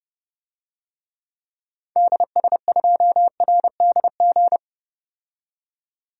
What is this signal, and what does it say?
Morse code, heard as a keyed tone, about 31 wpm, 700 Hz. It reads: DS2RDG
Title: DXCC-100 Call Signs - 6 Characters - 30wpm